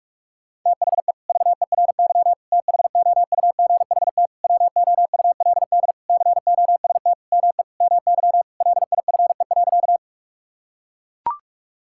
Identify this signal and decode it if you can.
Morse code, heard as a keyed tone, about 41 wpm, 700 Hz. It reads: THE VERY THOUGHT WOULD COST ME MY LIFE.
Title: Sentences from Top 1000 Words - Review Part 5 - 45wpm